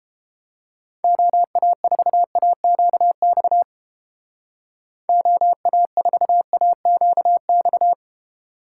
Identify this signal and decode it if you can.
Morse code, heard as a keyed tone, about 29 wpm, 700 Hz. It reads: OA4AQX OA4AQX